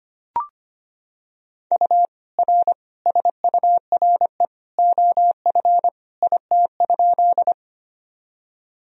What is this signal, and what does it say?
Morse code, heard as a keyed tone, about 25 wpm, 700 Hz. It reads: U R SURE OF IT?
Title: Sentences from Top 300 Words and Abbreviations 25wpm